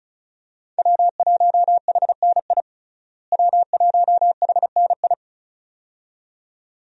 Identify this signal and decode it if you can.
Morse code, heard as a keyed tone, about 36 wpm, 700 Hz. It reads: W1HNI W1HNI
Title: USA Call Signs - 5 Characters - Repeated 2x - 35wpm